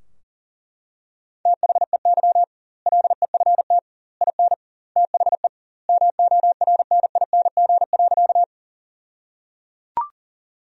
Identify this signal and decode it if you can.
Morse code, heard as a keyed tone, about 37 wpm, 700 Hz. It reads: THEY LEFT IN THE MORNING.
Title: 2x Word-Spacing - Sentences from Top 400 Words - 40wpm